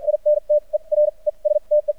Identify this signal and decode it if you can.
Morse code, heard as a keyed tone, about 51 wpm, spaced as for 38 wpm, 600 Hz. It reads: INTERESTE
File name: archived / CRSnet